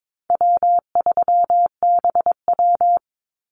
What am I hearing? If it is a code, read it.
Morse code, heard as a keyed tone, about 22 wpm, 700 Hz. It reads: W3BW
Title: Easy Call Signs Repeated 3x - 22wpm